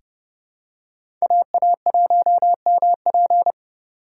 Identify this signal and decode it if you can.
Morse code, heard as a keyed tone, about 30 wpm, 700 Hz. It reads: AA1MP
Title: USA Call Signs - 5 Characters - 4x Character Spacing - 30wpm